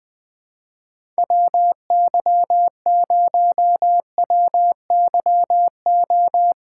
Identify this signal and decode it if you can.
Morse code, heard as a keyed tone, about 20 wpm, 700 Hz. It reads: WY0WYO
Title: SOTA QSO Practice 20wpm